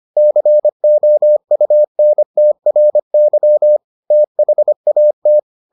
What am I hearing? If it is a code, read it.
Morse code, heard as a keyed tone, about 25 wpm, 600 Hz. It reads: COUNTRY THAT